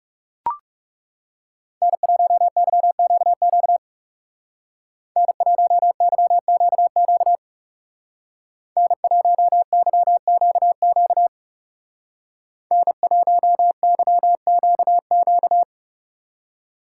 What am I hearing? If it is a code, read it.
Morse code, heard as a keyed tone, about 35 wpm, 700 Hz. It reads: N1YQQ N1YQQ N1YQQ N1YQQ
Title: Speed Racing - USA 1x3 Call Signs 30wpm